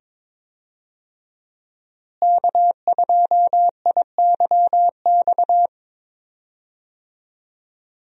Morse code, heard as a keyed tone, about 22 wpm, 700 Hz. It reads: K2IYX
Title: USA Call Signs - 5 Characters - Repeated 2x - 22wpm